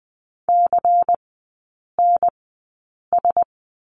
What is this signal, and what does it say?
Morse code, heard as a keyed tone, about 20 wpm, spaced as for 8 wpm, 700 Hz. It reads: CNS
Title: 127 Lesson 21e - Number 6 - Three Characters 20wpm